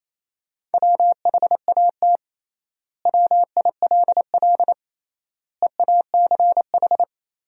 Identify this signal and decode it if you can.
Morse code, heard as a keyed tone, about 28 wpm, 700 Hz. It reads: WHAT WILL EACH
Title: 3x Word-Spacing - Sentences from Top 200 Words - 28wpm